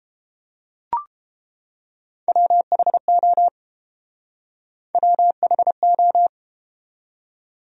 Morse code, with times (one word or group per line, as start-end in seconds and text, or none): 2.3-3.5 WHO
4.9-6.3 WHO